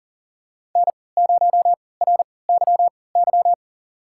Morse code, heard as a keyed tone, about 40 wpm, spaced as for 23 wpm, 700 Hz. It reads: N0RYY
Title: USA Call Signs - 3x Character Spacing - 40wpm